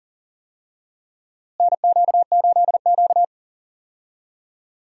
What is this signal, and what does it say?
Morse code, heard as a keyed tone, about 40 wpm, 700 Hz. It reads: NQ8Q
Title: USA Call Signs - 4 Characters - Repeated 2x - 40wpm